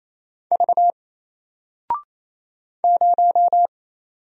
Morse code, heard as a keyed tone, about 27 wpm, 700 Hz. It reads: V 0
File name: Single-Letter-Number-28wpm